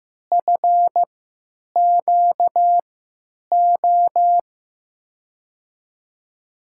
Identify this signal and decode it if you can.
Morse code, heard as a keyed tone, about 15 wpm, spaced as for 9 wpm, 700 Hz. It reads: FQO
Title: USA Call Signs - 3x Character Spacing - 15wpm